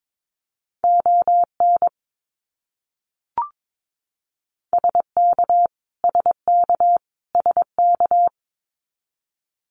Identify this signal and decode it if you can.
Morse code, heard as a keyed tone, about 22 wpm, 700 Hz. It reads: ON SK SK SK
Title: Canadian Provinces and Territories - Repeated 3x - 22wpm